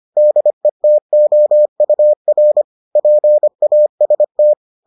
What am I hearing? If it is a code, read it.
Morse code, heard as a keyed tone, about 25 wpm, 600 Hz. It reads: DETOUR PAST